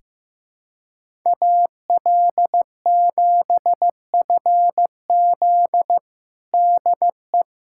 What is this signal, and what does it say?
Morse code, heard as a keyed tone, about 16 wpm, 700 Hz. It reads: AL7FZ DE
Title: USA Callsign DE Callsign - 5 Characters - 15wpm